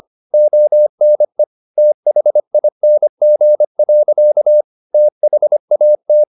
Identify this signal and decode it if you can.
Morse code, heard as a keyed tone, about 25 wpm, 600 Hz. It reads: ONE THING. THAT
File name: the old man and the sea part 1 25wpm